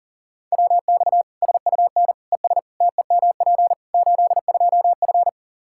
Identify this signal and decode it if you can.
Morse code, heard as a keyed tone, about 40 wpm, 700 Hz. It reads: WX SUN ES TEMP 82F
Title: K1USN SST QSO Practice 40wpm